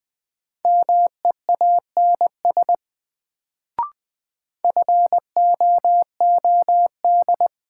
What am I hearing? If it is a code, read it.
Morse code, heard as a keyed tone, about 21 wpm, 700 Hz. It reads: MEANS FOOD
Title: Sets of 5 Words - New Word Drill - 20wpm